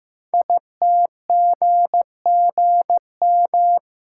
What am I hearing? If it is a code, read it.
Morse code, heard as a keyed tone, about 15 wpm, 700 Hz. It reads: ITGGM
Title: Code Groups - 5 Letters - 15wpm